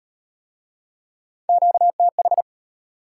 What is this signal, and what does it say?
Morse code, heard as a keyed tone, about 32 wpm, 690 Hz. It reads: QTH